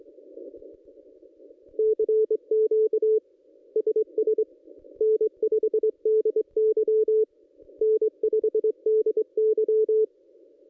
Morse code, heard as a keyed tone, about 25 wpm, 440 Hz. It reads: CQ SS N5DY N5DY
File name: contests / ss-cw-k5tr-op-n5rz -1900